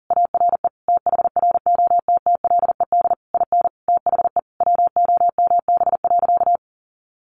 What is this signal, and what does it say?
Morse code, heard as a keyed tone, about 40 wpm, 700 Hz. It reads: ARE THROTTLED IN THE WOMB.
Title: A Sound Of Thunder 40wpm